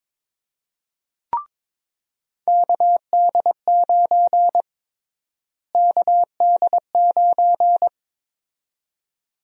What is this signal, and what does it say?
Morse code, heard as a keyed tone, about 22 wpm, 700 Hz. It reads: KD9 KD9
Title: USA Call Sign Prefixes - Repeated 2x - 22wpm